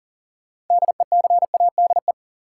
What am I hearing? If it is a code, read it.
Morse code, heard as a keyed tone, about 40 wpm, 700 Hz. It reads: DECADE
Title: Sets of 5 Words - New Word Drill - 40wpm